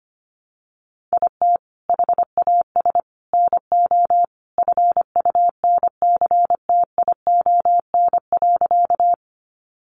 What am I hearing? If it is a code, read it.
Morse code, heard as a keyed tone, about 25 wpm, 700 Hz. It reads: IT HAS NO FUNCTION.
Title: Sentences from Top 1100 Words 25wpm